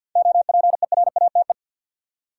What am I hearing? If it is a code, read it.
Morse code, heard as a keyed tone, about 53 wpm, 700 Hz. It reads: OPERATE
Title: Seven Letter Words - 50wpm